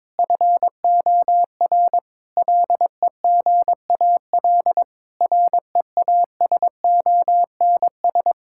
Morse code, heard as a keyed tone, about 22 wpm, 700 Hz. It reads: FOR LEGAL REASONS
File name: practice / 22wpm / Sets of 8 Words - 22wpm